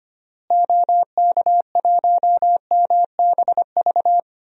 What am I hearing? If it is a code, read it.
Morse code, heard as a keyed tone, about 25 wpm, 700 Hz. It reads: OK1MBV
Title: DXCC-100 Call Signs - 6 Characters - Repeated 3x - 25wpm